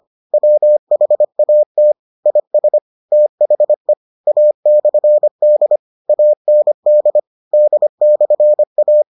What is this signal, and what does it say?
Morse code, heard as a keyed tone, about 25 wpm, 600 Hz. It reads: WHAT IS THE A/D AND D/A